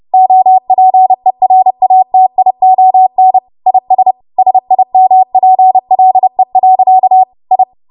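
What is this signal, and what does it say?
Morse code, heard as a keyed tone, about 30 wpm, 750 Hz. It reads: OPERATION IS SIMPLE. I
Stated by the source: Morse practice transmission at 30 WPM.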